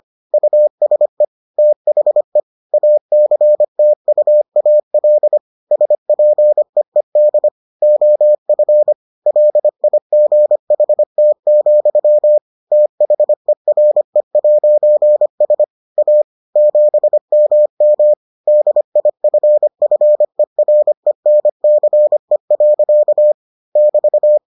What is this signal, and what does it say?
Morse code, heard as a keyed tone, about 25 wpm, 600 Hz. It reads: USE THE ACTUAL SPEED OF LIGHT, THERE'S A 7MM DIFFERENCE. <BT>